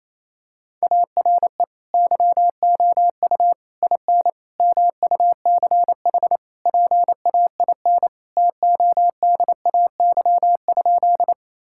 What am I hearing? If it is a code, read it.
Morse code, heard as a keyed tone, about 28 wpm, 700 Hz. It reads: ARE YOU IN MUCH PAIN TODAY?
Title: Sentences from Top 1600 Words 28wpm